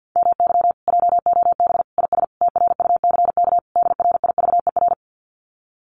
Morse code, heard as a keyed tone, about 50 wpm, 700 Hz. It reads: MY JOB IS TRUCK DRIVER
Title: Common QSO Phrases Encore 50wpm